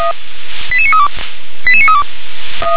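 Five-tone selective call (ZVEI), heard as quick runs of five tones, tones of about 70 ms. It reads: 88932 88932
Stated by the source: ZVEI five-tone selective call 88982, received by SDR.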